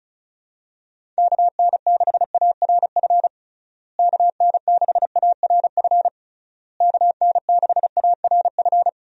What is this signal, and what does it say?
Morse code, heard as a keyed tone, about 34 wpm, 700 Hz. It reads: KN6ARF KN6ARF KN6ARF
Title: USA Call Signs - 6 Characters - Repeated 3x - 35wpm